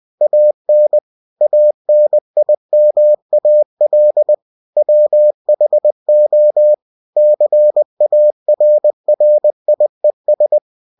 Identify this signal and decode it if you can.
Morse code, heard as a keyed tone, about 20 wpm, 600 Hz. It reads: AN ANIMAL WHO CARRIES